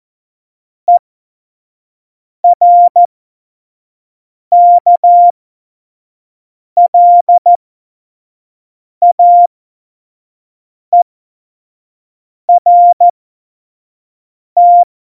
Morse code, heard as a keyed tone, about 14 wpm, spaced as for 5 wpm, 700 Hz. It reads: ERKLAERT
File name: Morsekurs / Witze / Witz 19. September 17, Tempo 25